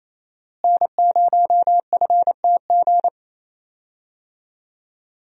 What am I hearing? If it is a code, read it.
Morse code, heard as a keyed tone, about 28 wpm, 700 Hz. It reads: N0FTG
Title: USA Call Signs - 5 Characters - Repeated 2x - 28wpm